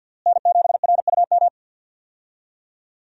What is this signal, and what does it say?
Morse code, heard as a keyed tone, about 50 wpm, 700 Hz. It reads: N7RUM
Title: USA Call Signs - 5 Characters - Repeated 2x - 50wpm